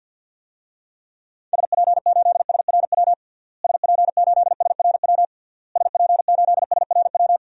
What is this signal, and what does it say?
Morse code, heard as a keyed tone, about 51 wpm, 700 Hz. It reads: SP8SRW SP8SRW SP8SRW
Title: DXCC-100 Call Signs - 6 Characters - Repeated 3x - 50wpm